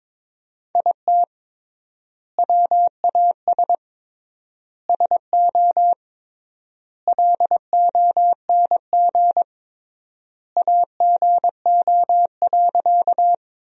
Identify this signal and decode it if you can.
Morse code, heard as a keyed tone, about 22 wpm, 700 Hz. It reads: IT WAS SO LONG AGO.
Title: 3x Word-Spacing - Sentences from Top 500 Words - 22wpm